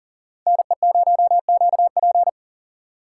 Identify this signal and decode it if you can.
Morse code, heard as a keyed tone, about 40 wpm, 700 Hz. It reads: NE0QP
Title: USA Call Signs - 5 Characters - 6x Character Spacing - 40wpm